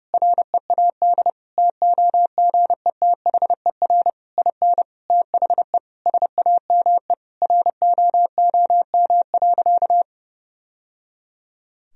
Morse code, heard as a keyed tone, about 32 wpm, 700 Hz. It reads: READ TOGETHER IN THE SAME ROOM.
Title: Sentences from Top 1000 Words - Review Part 3 - 30wpm